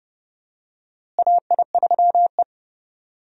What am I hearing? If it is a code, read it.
Morse code, heard as a keyed tone, about 30 wpm, 700 Hz. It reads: AI3E
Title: USA Call Signs - 4 Characters - Repeated 3x - 30wpm